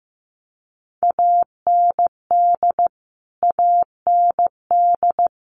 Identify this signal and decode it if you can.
Morse code, heard as a keyed tone, about 15 wpm, 700 Hz. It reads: AND AND
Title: Top 500 Words Repeated 15wpm